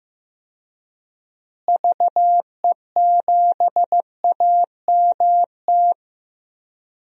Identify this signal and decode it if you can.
Morse code, heard as a keyed tone, about 15 wpm, 700 Hz. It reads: VE7AMT
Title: DXCC-100 Call Signs - 6 Characters - Repeated 2x - 15wpm